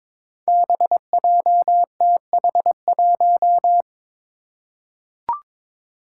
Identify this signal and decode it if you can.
Morse code, heard as a keyed tone, about 22 wpm, 700 Hz. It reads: BJTH1
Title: Five Challenging Letter-Number Combinations 22wpm